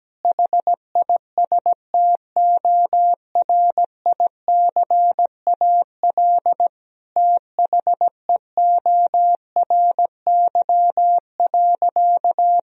Morse code, heard as a keyed tone, about 17 wpm, 700 Hz. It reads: HISTORICAL THEORY.
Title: Sentences from Top 1000 Words - Review Part 4 - 17wpm